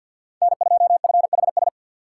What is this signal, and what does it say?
Morse code, heard as a keyed tone, about 53 wpm, 700 Hz. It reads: N2FHS
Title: USA Call Signs - 3x Character Spacing - 50wpm